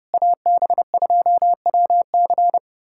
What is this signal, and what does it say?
Morse code, heard as a keyed tone, about 30 wpm, 700 Hz. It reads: AB2WC
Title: USA Callsign DE Callsign - 5 Characters - 30wpm